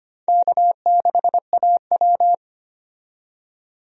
Morse code, heard as a keyed tone, about 25 wpm, 700 Hz. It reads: K6AW
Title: USA Call Signs - 4 Characters - 25wpm